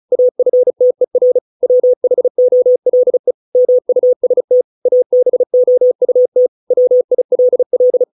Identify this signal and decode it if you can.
Morse code, heard as a keyed tone, about 34 wpm, 500 Hz. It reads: AFTER WHOLE MUST ABOUT WILL